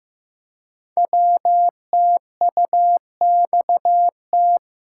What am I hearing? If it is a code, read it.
Morse code, heard as a keyed tone, about 15 wpm, 700 Hz. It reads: WTUXT
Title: Code Groups - 5 Letters - 15wpm